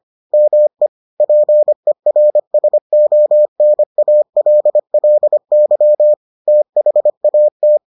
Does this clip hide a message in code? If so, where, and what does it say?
Yes: Morse code from 0.3 to 7.8 s, heard as a keyed tone, about 25 wpm, 600 Hz. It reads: ME PERSONALLY THAT